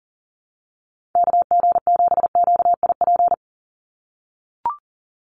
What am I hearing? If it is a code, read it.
Morse code, heard as a keyed tone, about 40 wpm, 700 Hz. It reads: KG7QIP
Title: USA Call Signs 40wpm